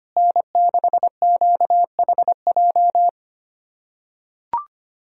Morse code, heard as a keyed tone, about 25 wpm, 700 Hz. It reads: N6QHJ
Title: USA Callsign DE Callsign - 5 Characters - 25wpm